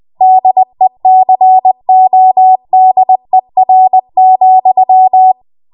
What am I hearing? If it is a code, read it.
Morse code, heard as a keyed tone, about 20 wpm, 750 Hz. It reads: DECODER,
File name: Morse / 20WPM